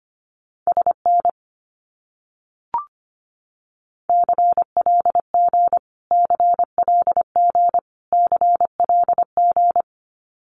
Morse code, heard as a keyed tone, about 25 wpm, 700 Hz. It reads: SN CLG CLG CLG
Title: Abbreviations Repeated 3x - 25wpm